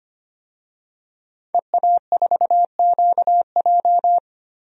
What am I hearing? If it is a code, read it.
Morse code, heard as a keyed tone, about 25 wpm, 700 Hz. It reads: EA4QJ
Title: DXCC-100 Call Signs - 5 Characters - 25wpm